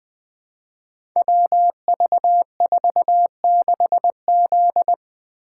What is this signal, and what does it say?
Morse code, heard as a keyed tone, about 20 wpm, 700 Hz. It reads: WV46Z
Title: Code Groups - 5 Letter:Numbers - 20wpm